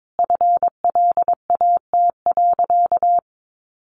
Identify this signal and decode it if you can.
Morse code, heard as a keyed tone, about 22 wpm, 700 Hz. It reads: FLAT.
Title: Sentences from Top 1000 Words 22wpm